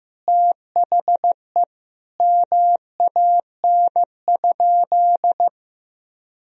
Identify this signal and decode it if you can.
Morse code, heard as a keyed tone, about 16 wpm, 700 Hz. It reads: THE MAN?
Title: Sentences from Top 200 Words and Abbreviations 15wpm